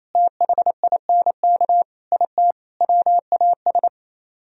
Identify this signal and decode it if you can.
Morse code, heard as a keyed tone, about 36 wpm, 700 Hz. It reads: THINK IT WAS